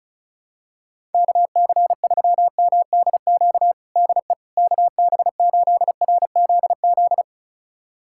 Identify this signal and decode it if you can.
Morse code, heard as a keyed tone, about 36 wpm, 700 Hz. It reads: KC3MDQ DE KB8RZZ
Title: USA Callsign DE Callsign 35wpm